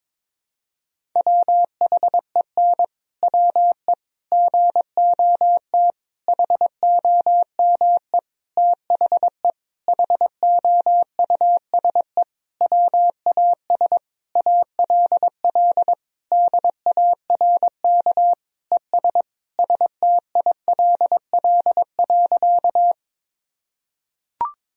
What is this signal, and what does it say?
Morse code, heard as a keyed tone, about 22 wpm, 700 Hz. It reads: WHEN WE GOT HOME THE HOUSE WAS ALL DARK ES STILL.
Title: Sentences from Top 500 Words and Abbreviations 22wpm